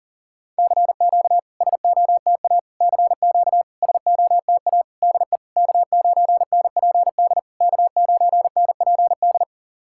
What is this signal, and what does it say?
Morse code, heard as a keyed tone, about 44 wpm, 690 Hz. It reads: CQ SOTA CQ SOTA DE K9NPD K9NPD
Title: SOTA QSO Practice 40wpm